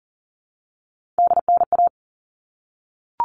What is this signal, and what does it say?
Morse code, heard as a keyed tone, about 40 wpm, 700 Hz. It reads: DNA
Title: All Cut Numbers - 3-Digit 40wpm